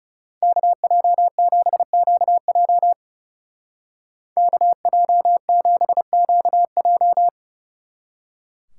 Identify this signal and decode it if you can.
Morse code, heard as a keyed tone, about 38 wpm, 700 Hz. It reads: KJ7QJ KJ7QJ
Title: Speed Racing - USA Call Signs 30wpm